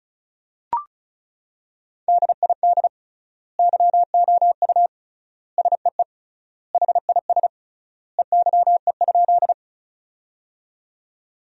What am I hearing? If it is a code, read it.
Morse code, heard as a keyed tone, about 36 wpm, 700 Hz. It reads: DID YOU SEE HIS EYE?
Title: 3x Word-Spacing - Sentences from Top 300 Words - 35wpm